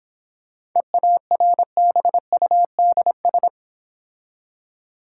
Morse code, heard as a keyed tone, about 26 wpm, 690 Hz. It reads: EARBUDS